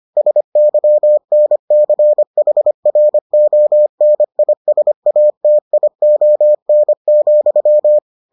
Morse code, heard as a keyed tone, about 25 wpm, 600 Hz. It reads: SYNCHRONISATION,